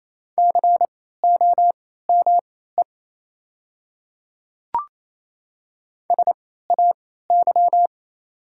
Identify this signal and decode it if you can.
Morse code, heard as a keyed tone, about 28 wpm, spaced as for 16 wpm, 700 Hz. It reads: COME SAY
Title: 3x Character Spacing - Top 200 Words - 28wpm